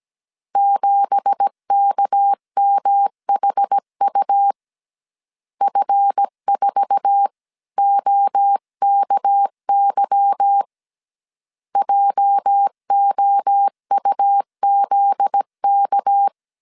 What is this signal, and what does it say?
Morse code, heard as a keyed tone, about 17 wpm, 800 Hz. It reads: 7KMHU F4OKY JOUZK
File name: CW / DET / 1 9 1500C S C 16wpm 2909